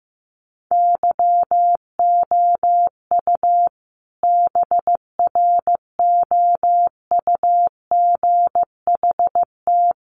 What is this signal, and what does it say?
Morse code, heard as a keyed tone, about 15 wpm, 700 Hz. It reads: YOU BROUGHT
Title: Sentences from Top 1300 Words 15wpm